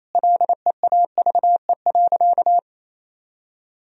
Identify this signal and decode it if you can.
Morse code, heard as a keyed tone, about 28 wpm, 700 Hz. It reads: LEAVE.
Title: Sentences from Top 1000 Words - Review Part 4 - 28wpm